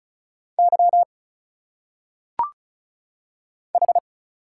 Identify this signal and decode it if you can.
Morse code, heard as a keyed tone, about 34 wpm, 700 Hz. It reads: Y H
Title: Single Letters - Phonetic Pronunciation 35wpm